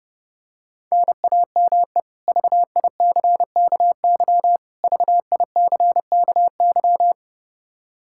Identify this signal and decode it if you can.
Morse code, heard as a keyed tone, about 30 wpm, 700 Hz. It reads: NAME VICKY VICKY
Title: K1USN SST QSO Practice 30wpm